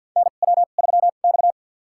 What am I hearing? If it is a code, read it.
Morse code, heard as a keyed tone, about 52 wpm, spaced as for 37 wpm, 700 Hz. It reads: NW3X
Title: USA Call Signs - 4 Characters - 2x Character Spacing - 50wpm